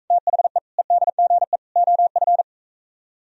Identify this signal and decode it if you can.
Morse code, heard as a keyed tone, about 45 wpm, 700 Hz. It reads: THE EDGE OF